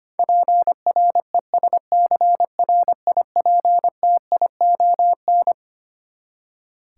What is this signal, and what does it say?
Morse code, heard as a keyed tone, about 25 wpm, 700 Hz. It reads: PRESCRIPTION
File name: Twelve Letter Words - 25wpm